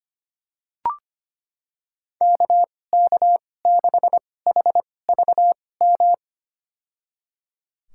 Morse code, heard as a keyed tone, about 25 wpm, spaced as for 18 wpm, 700 Hz. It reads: KK6HVM
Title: USA Call Signs - 6 Characters - 2x Character Spacing - 25wpm